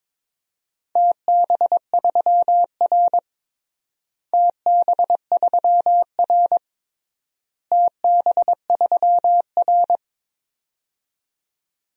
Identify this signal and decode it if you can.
Morse code, heard as a keyed tone, about 22 wpm, 700 Hz. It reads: TB3R TB3R TB3R
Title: DXCC-100 Call Signs - 4 Characters - Repeated 3x - 22wpm